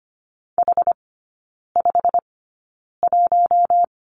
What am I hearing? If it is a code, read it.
Morse code, heard as a keyed tone, about 25 wpm, spaced as for 9 wpm, 690 Hz. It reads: H51
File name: Lesson 15e - Number 5 - Three Characters 25wpm